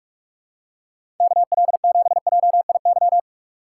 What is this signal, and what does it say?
Morse code, heard as a keyed tone, about 46 wpm, 700 Hz. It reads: KL7JIY
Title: USA Call Signs - 3x Character Spacing - 45wpm